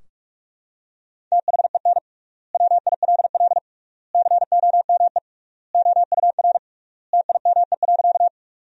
Morse code, heard as a keyed tone, about 41 wpm, 700 Hz. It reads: THEN WILL COME OUR TIME.
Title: 3x Word-Spacing - Sentences from Top 200 Words - 45wpm